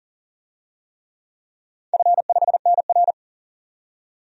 Morse code, heard as a keyed tone, about 42 wpm, 700 Hz. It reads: F5NR
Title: DXCC-100 Call Signs - 4 Characters - 40wpm